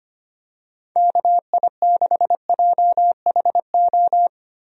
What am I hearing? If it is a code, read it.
Morse code, heard as a keyed tone, about 25 wpm, 700 Hz. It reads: KI6JHO
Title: USA Call Signs - 6 Characters - Repeated 2x - 25wpm